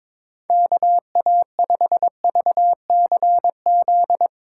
Morse code, heard as a keyed tone, about 22 wpm, 700 Hz. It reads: KA5VCZ
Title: USA Call Signs - 6x Character Spacing - 22wpm